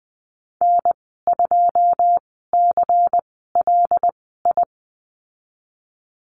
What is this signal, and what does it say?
Morse code, heard as a keyed tone, about 20 wpm, spaced as for 14 wpm, 700 Hz. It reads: N2CLI
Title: 154 Lesson 25g - Letter p - Call Signs 20wpm